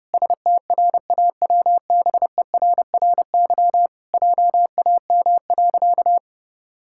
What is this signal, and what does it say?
Morse code, heard as a keyed tone, about 30 wpm, 700 Hz. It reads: STRAWBERRY JAM.
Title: Pangram Challenge v2 - 30wpm